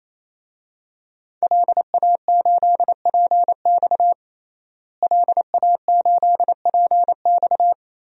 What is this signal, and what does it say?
Morse code, heard as a keyed tone, about 28 wpm, 700 Hz. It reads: LA8PX LA8PX
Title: DXCC-100 Call Signs - 5 Characters - Repeated 3x - 28wpm